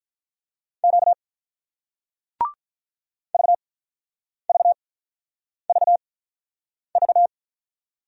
Morse code, heard as a keyed tone, about 48 wpm, spaced as for 9 wpm, 700 Hz. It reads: Q VVVV